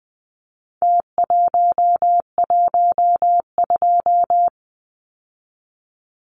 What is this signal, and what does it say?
Morse code, heard as a keyed tone, about 20 wpm, 700 Hz. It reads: T112
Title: Numbers - Contest Style - 4-Digit 20wpm